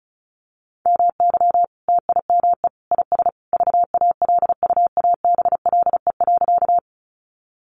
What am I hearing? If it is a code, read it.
Morse code, heard as a keyed tone, about 34 wpm, 700 Hz. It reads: MY TIME IS VALUABLE.
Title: Sentences from Top 1100 Words 35wpm